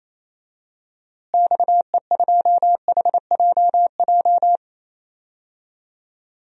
Morse code, heard as a keyed tone, about 28 wpm, 700 Hz. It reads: XE2HJJ
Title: DXCC-100 Call Signs - 6 Characters - Repeated 3x - 28wpm